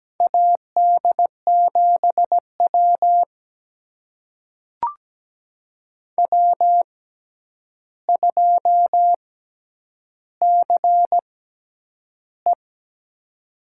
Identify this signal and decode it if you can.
Morse code, heard as a keyed tone, about 17 wpm, spaced as for 9 wpm, 700 Hz. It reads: AD7W W2CE
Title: USA Call Signs - 4 Characters - 6x Character Spacing - 17wpm